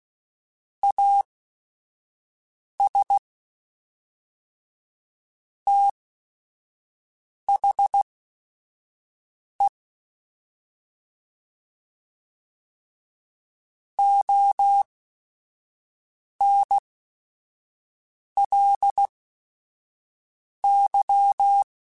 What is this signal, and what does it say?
Morse code, heard as a keyed tone, about 16 wpm, spaced as for 5 wpm, 800 Hz. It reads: AS THE ONLY
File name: Technical Subjects / Technical Subjects-mp3s / AND Gate @5